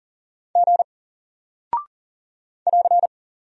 Morse code, heard as a keyed tone, about 40 wpm, 700 Hz. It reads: G <AR>
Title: Single Character (Letter:Number:Punctuation:Prosign) 40wpm